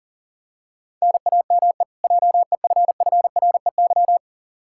Morse code, heard as a keyed tone, about 40 wpm, 700 Hz. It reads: NAME JEFFREY
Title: Practice QSO - Exchange 1 - Beginning - 40wpm